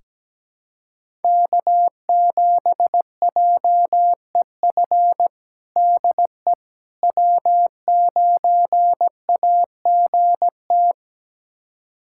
Morse code, heard as a keyed tone, about 17 wpm, 700 Hz. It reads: K7JEF DE W9AGT
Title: USA Callsign DE Callsign - 5 Characters - 17wpm